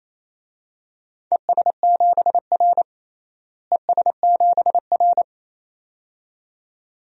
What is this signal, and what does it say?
Morse code, heard as a keyed tone, about 28 wpm, 700 Hz. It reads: ES7R ES7R
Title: DXCC-100 Call Signs - 4 Characters - Repeated 2x - 28wpm